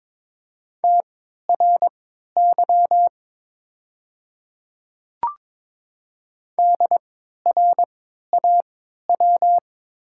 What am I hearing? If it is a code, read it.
Morse code, heard as a keyed tone, about 22 wpm, spaced as for 12 wpm, 700 Hz. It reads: TRY DRAW
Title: 3x Character Spacing - Top 500 Words - 22wpm